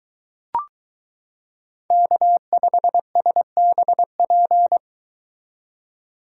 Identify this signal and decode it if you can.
Morse code, heard as a keyed tone, about 23 wpm, 700 Hz. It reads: K5SBP